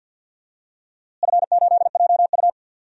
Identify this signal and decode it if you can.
Morse code, heard as a keyed tone, about 45 wpm, 700 Hz. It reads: F8JU
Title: DXCC-100 Call Signs - 4 Characters - Repeated 2x - 50wpm